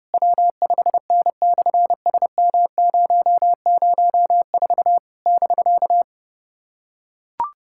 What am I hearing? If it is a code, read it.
Morse code, heard as a keyed tone, about 30 wpm, 700 Hz. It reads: W5N/SM004 <BK>
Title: SOTA QSO Practice 30wpm